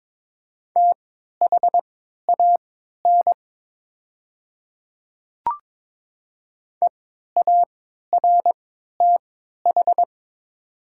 Morse code, heard as a keyed tone, about 22 wpm, spaced as for 12 wpm, 700 Hz. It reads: THAN EARTH
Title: 3x Character Spacing - Top 200 Words - 22wpm